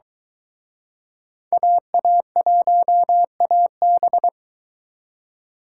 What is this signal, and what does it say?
Morse code, heard as a keyed tone, about 23 wpm, 700 Hz. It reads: AA1AB